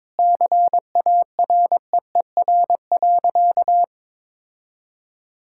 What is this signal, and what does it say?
Morse code, heard as a keyed tone, about 21 wpm, 700 Hz. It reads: CAREER.
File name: Sentences with 7-Word Phrases - 22wpm